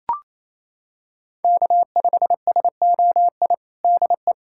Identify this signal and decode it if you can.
Morse code, heard as a keyed tone, about 28 wpm, 700 Hz. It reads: K5SOI DE
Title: USA Callsign DE Callsign 28wpm